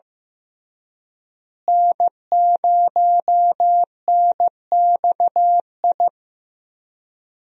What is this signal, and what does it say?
Morse code, heard as a keyed tone, about 15 wpm, 700 Hz. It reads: N0NXI
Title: USA Call Signs - 5 Characters - Repeated 3x - 15wpm